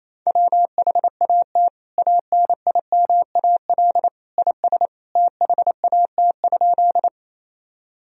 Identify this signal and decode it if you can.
Morse code, heard as a keyed tone, about 28 wpm, 700 Hz. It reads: WHAT ANIMAL IS THAT?
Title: Sentences from Top 1000 Words - Review Part 1 - 28wpm